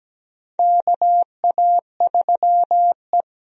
Morse code, heard as a keyed tone, about 17 wpm, 700 Hz. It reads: KA3E
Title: USA Call Signs - 4 Characters - 17wpm